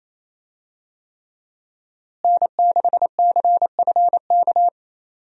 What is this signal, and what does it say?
Morse code, heard as a keyed tone, about 28 wpm, 700 Hz. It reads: N6CFK
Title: USA Call Signs - 5 Characters - Repeated 3x - 28wpm